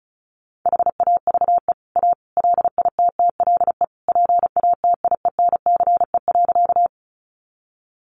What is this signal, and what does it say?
Morse code, heard as a keyed tone, about 34 wpm, 700 Hz. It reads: HAVE A LITTLE PATIENCE.
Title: Sentences from Top 1300 Words 35wpm